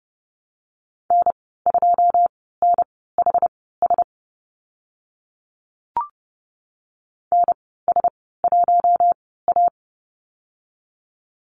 Morse code, heard as a keyed tone, about 30 wpm, spaced as for 17 wpm, 700 Hz. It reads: N2NHS NS1A
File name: Lesson 14g - Number 2 - Call Signs 30wpm